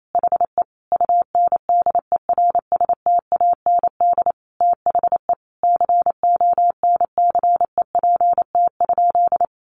Morse code, heard as a keyed tone, about 28 wpm, 700 Hz. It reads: HE UNDERSTAND THE CONCEPT?
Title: Sentences from Top 1100 Words 28wpm